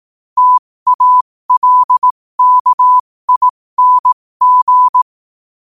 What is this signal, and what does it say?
Morse code, heard as a keyed tone, about 18 wpm, spaced as for 15 wpm, 1000 Hz. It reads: TALKING
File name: qotdfiles / 15wpm